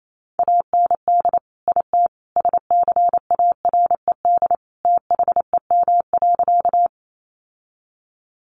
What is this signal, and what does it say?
Morse code, heard as a keyed tone, about 28 wpm, 700 Hz. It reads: AND IT SCARED THEM.
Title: Sentences from Top 700 Words 28wpm